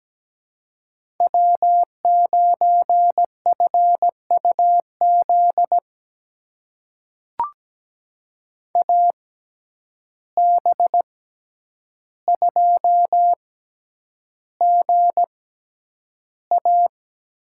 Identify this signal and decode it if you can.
Morse code, heard as a keyed tone, about 17 wpm, spaced as for 9 wpm, 700 Hz. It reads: W9FUZ AB2GA
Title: USA Call Signs - 5 Characters - 6x Character Spacing - 17wpm